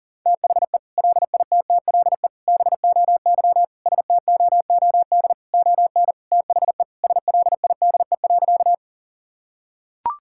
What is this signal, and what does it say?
Morse code, heard as a keyed tone, about 40 wpm, 700 Hz. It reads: THE LITTLE BOY STOOD ON THE SLIDE.
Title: Sentences from Top 1600 Words 40wpm